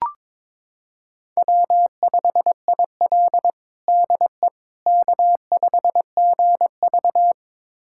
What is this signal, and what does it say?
Morse code, heard as a keyed tone, about 22 wpm, 700 Hz. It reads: W5IL DE K5GV
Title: USA Callsign DE Callsign - 4 Characters - 22wpm